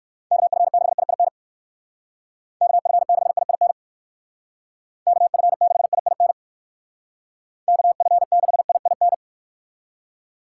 Morse code, heard as a keyed tone, about 58 wpm, 700 Hz. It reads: KF6IIN KF6IIN KF6IIN KF6IIN